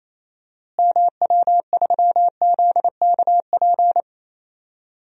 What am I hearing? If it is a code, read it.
Morse code, heard as a keyed tone, about 27 wpm, 700 Hz. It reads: MW3ZKP
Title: DXCC-100 Call Signs - 6 Characters - Repeated 2x - 28wpm